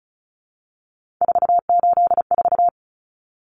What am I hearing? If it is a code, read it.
Morse code, heard as a keyed tone, about 34 wpm, 700 Hz. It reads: 484
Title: Numbers - 3-Digit 35wpm